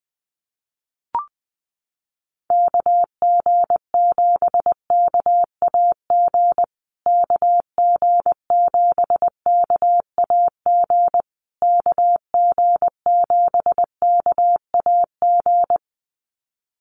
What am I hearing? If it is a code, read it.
Morse code, heard as a keyed tone, about 19 wpm, 700 Hz. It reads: KG7KAG KG7KAG KG7KAG
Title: USA Call Signs Repeated 3x 20wpm